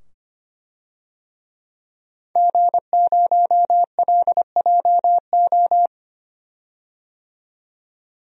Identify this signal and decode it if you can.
Morse code, heard as a keyed tone, about 25 wpm, 700 Hz. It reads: G0LJO
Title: DXCC-100 Call Signs - 5 Characters - 25wpm